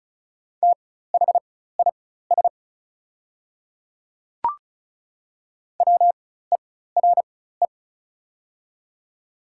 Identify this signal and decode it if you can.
Morse code, heard as a keyed tone, about 36 wpm, spaced as for 16 wpm, 700 Hz. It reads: THIS WERE
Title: 4x Character Spacing - Top 100 Words - 35wpm